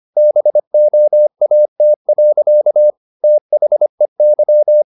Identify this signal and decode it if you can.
Morse code, heard as a keyed tone, about 25 wpm, 600 Hz. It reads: BOAT. THEY
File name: the old man and the sea part 1 25wpm